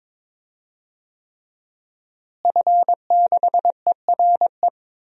Morse code, heard as a keyed tone, about 22 wpm, 700 Hz. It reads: F6ERE
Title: DXCC-100 Call Signs - 5 Characters - Repeated 3x - 22wpm